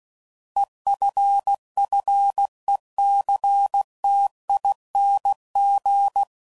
Morse code, heard as a keyed tone, about 16 wpm, 800 Hz. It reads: EFFECTING